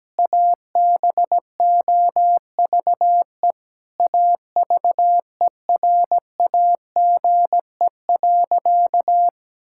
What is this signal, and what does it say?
Morse code, heard as a keyed tone, about 17 wpm, 700 Hz. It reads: ABOVE AVERAGE.
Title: Sentences from Top 1000 Words - Review Part 6 - 17wpm